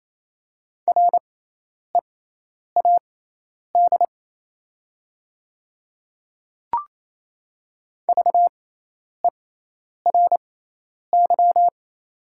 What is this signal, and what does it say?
Morse code, heard as a keyed tone, about 28 wpm, spaced as for 10 wpm, 700 Hz. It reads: READ VERY
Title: 6x Character Spacing - Top 200 Words - 28wpm